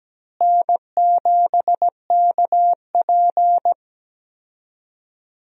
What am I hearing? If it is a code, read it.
Morse code, heard as a keyed tone, about 17 wpm, 700 Hz. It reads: N7KP
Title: USA Call Signs - 4 Characters - Repeated 2x - 17wpm